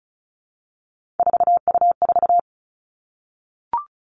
Morse code, heard as a keyed tone, about 36 wpm, 700 Hz. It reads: 4U4
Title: Three Challenging Letter-Number Combinations 35wpm